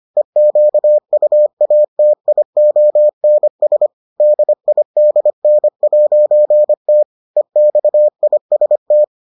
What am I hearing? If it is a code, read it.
Morse code, heard as a keyed tone, about 25 wpm, 600 Hz. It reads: EQUATIONS DIDN'T EXIST